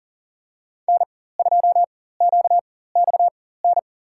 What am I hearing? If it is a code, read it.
Morse code, heard as a keyed tone, about 38 wpm, spaced as for 18 wpm, 700 Hz. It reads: N2QXN
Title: USA Call Signs - 5 Characters - 4x Character Spacing - 40wpm